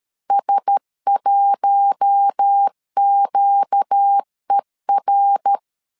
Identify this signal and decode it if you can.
Morse code, heard as a keyed tone, about 13 wpm, 800 Hz. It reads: S1QER
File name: LEZIONI / 4 Q1S 12wpm 1802